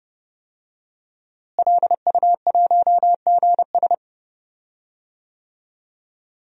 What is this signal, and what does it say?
Morse code, heard as a keyed tone, about 31 wpm, 700 Hz. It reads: LU1GS
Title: DXCC-100 Call Signs - 5 Characters - Repeated 3x - 30wpm